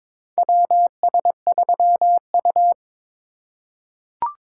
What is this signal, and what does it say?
Morse code, heard as a keyed tone, about 22 wpm, 700 Hz. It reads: WS3U
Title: USA Call Signs - 4 Characters - 2x Character Spacing - 22wpm